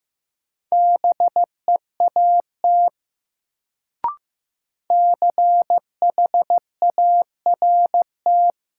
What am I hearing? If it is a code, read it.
Morse code, heard as a keyed tone, about 15 wpm, 700 Hz. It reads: BEAT CHART
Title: Sentences from Top 1300 Words - New Word Drill Alt - 15wpm